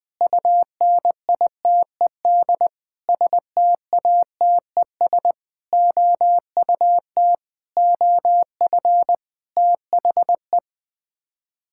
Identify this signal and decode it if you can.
Morse code, heard as a keyed tone, about 20 wpm, 700 Hz. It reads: UNITED STATES OUT OF THE
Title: Sets of 8 Words - 20wpm